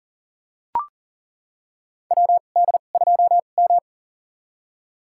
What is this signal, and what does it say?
Morse code, heard as a keyed tone, about 40 wpm, spaced as for 29 wpm, 700 Hz. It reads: WD2M
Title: USA Call Signs - 4 Characters - 2x Character Spacing - 40wpm